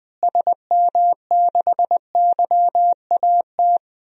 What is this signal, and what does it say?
Morse code, heard as a keyed tone, about 20 wpm, 700 Hz. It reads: SM6YAT
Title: DXCC-100 Call Signs - 6 Characters - Repeated 2x - 20wpm